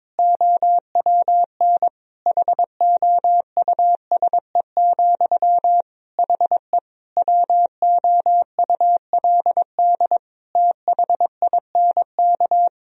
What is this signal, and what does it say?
Morse code, heard as a keyed tone, about 22 wpm, 700 Hz. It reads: OWN HOUSE, HE WOULD THINK
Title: Sentences from Top 1000 Words - Review Part 2 - 22wpm